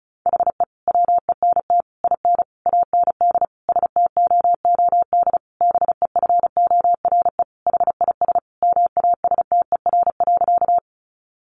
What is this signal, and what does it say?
Morse code, heard as a keyed tone, about 34 wpm, 700 Hz. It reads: HE WENT IN AND STOOD BEFORE HIS MASTER.
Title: Sentences from Top 1300 Words 35wpm